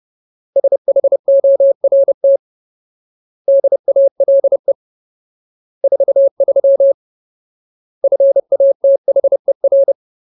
Morse code, heard as a keyed tone, about 30 wpm, 550 Hz. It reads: SHORT DALE 43 FATHER
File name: Ultimate Word List / 30WPM Ultimate